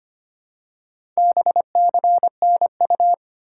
Morse code, heard as a keyed tone, about 25 wpm, 700 Hz. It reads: BCNU
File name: Top 90 QSO Elements - 2x Character Spacing - 25wpm